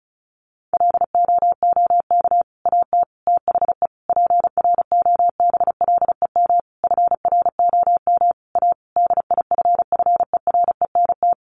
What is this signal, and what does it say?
Morse code, heard as a keyed tone, about 34 wpm, 700 Hz. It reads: LOOK AT THE PROBLEM FROM A DIFFERENT
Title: Sentences from Top 1500 Words 35wpm